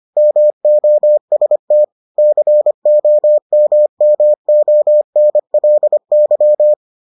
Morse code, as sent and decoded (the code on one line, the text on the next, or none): -- --- ... - / -.-. --- -- -- --- -. .-.. -.--
MOST COMMONLY